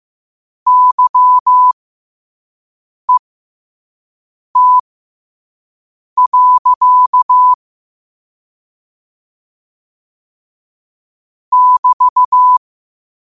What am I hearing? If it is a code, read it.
Morse code, heard as a keyed tone, about 15 wpm, spaced as for 5 wpm, 1000 Hz. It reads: YET. <BT>
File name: qotdfiles / 5wpm